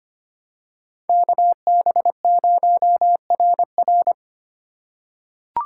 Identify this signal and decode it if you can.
Morse code, heard as a keyed tone, about 25 wpm, 700 Hz. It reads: KB0RR
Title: USA Call Signs - 2x Character Spacing - 25wpm